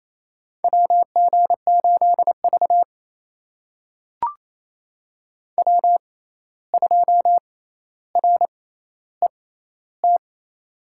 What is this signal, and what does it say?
Morse code, heard as a keyed tone, about 28 wpm, spaced as for 13 wpm, 700 Hz. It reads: WG8V W2RET
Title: USA Call Signs - 6x Character Spacing - 28wpm